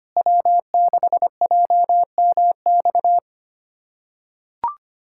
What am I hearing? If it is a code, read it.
Morse code, heard as a keyed tone, about 25 wpm, 700 Hz. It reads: W6JMX
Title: USA Callsign DE Callsign - 5 Characters - 25wpm